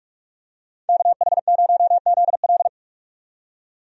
Morse code, heard as a keyed tone, about 46 wpm, 700 Hz. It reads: KH0ZL
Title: USA Call Signs - 5 Characters - 45wpm